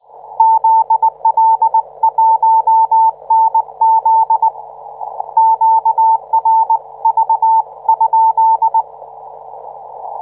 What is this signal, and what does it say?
Morse code, heard as a keyed tone, about 19 wpm, 890 Hz. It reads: ZL1NZ QRV?